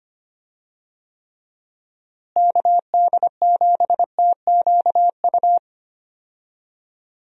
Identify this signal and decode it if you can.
Morse code, heard as a keyed tone, about 25 wpm, 700 Hz. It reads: KD7TQU
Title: USA Call Signs - 6 Characters - Repeated 3x - 25wpm